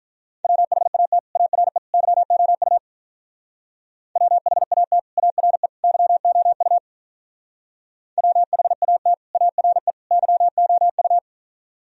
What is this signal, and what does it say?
Morse code, heard as a keyed tone, about 46 wpm, 700 Hz. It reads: WHAT ARE YOU WHAT ARE YOU WHAT ARE YOU